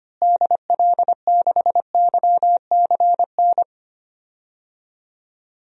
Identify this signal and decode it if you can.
Morse code, heard as a keyed tone, about 25 wpm, 700 Hz. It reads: DL6YCN
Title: DXCC-100 Call Signs - 6 Characters - Repeated 3x - 25wpm